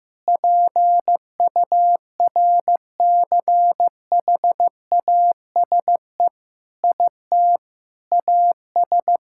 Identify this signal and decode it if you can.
Morse code, heard as a keyed tone, about 15 wpm, 700 Hz. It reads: PURCHASE IT AS
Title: Sentences from Top 1000 Words - Review Part 5 - 15wpm